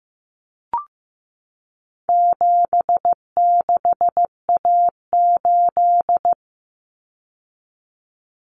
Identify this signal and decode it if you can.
Morse code, heard as a keyed tone, about 15 wpm, 700 Hz. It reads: 76A8
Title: Common Cut Numbers - 4-Digit 15wpm